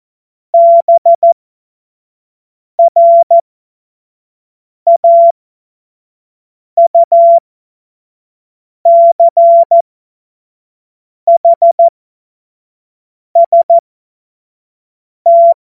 Morse code, heard as a keyed tone, about 14 wpm, spaced as for 5 wpm, 680 Hz. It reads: BRAUCHST
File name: Morsekurs / Witze / Witz 01. Juni 2021, 25 F70